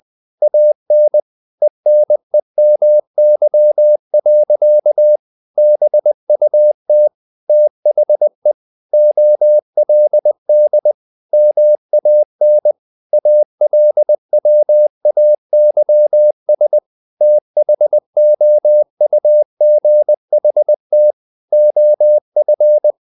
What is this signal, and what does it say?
Morse code, heard as a keyed tone, about 20 wpm, 600 Hz. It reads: AN ENEMY. BUT THE OLD MAN ALWAYS THOUGHT OF
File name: the old man and the sea / the old man and the sea part 1 20wpm